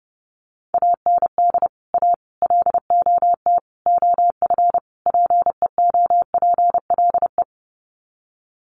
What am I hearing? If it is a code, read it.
Morse code, heard as a keyed tone, about 30 wpm, 700 Hz. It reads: AND A LOT OF PEOPLE
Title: Sets of 5 Words 30wpm